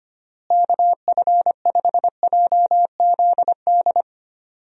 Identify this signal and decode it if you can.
Morse code, heard as a keyed tone, about 25 wpm, 700 Hz. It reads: KF5JZD
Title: USA Call Signs - 6 Characters - Repeated 2x - 25wpm